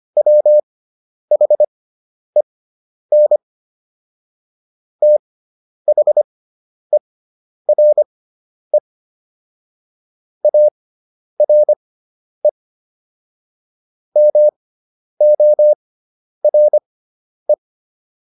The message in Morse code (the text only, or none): WHEN THERE ARE MORE